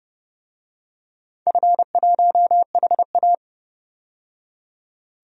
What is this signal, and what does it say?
Morse code, heard as a keyed tone, about 30 wpm, 700 Hz. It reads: F1HA
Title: DXCC-100 Call Signs - 4 Characters - 30wpm